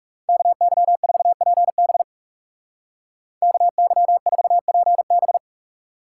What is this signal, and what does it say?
Morse code, heard as a keyed tone, about 38 wpm, 700 Hz. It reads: KY4PB KY4PB